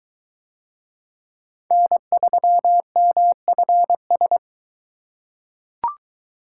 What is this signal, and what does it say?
Morse code, heard as a keyed tone, about 23 wpm, 700 Hz. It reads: N3MFS